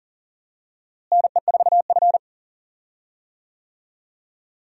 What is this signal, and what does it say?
Morse code, heard as a keyed tone, about 40 wpm, 700 Hz. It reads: NE4F
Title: USA Call Signs - 4 Characters - Repeated 3x - 40wpm